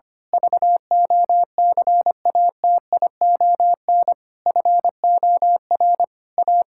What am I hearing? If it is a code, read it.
Morse code, heard as a keyed tone, about 25 wpm, 700 Hz. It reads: VOCATION FOR A
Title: Pangram Challenge v2 - 25wpm